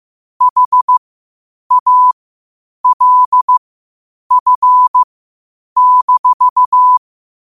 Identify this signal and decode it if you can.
Morse code, heard as a keyed tone, about 15 wpm, spaced as for 8 wpm, 1000 Hz. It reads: HALF-
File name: qotdfiles / 8wpm